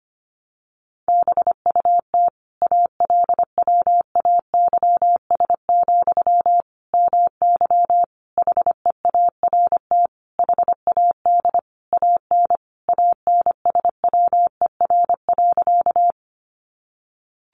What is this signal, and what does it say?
Morse code, heard as a keyed tone, about 25 wpm, 700 Hz. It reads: BUT ALWAYS, MY HEART HAD AN ANSWER.
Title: Sentences from Top 1200 Words 25wpm